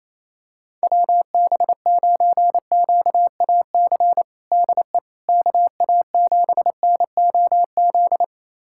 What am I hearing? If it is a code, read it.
Morse code, heard as a keyed tone, about 28 wpm, 700 Hz. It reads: WB9QAC DE KA7NOZ
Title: USA Callsign DE Callsign - 6 Characters - 28wpm